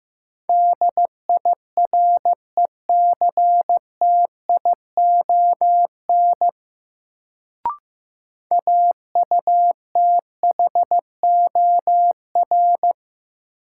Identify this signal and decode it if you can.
Morse code, heard as a keyed tone, about 15 wpm, 700 Hz. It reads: DIRECTION AUTHOR
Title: Sets of 4 Words - New Word Drill - 15wpm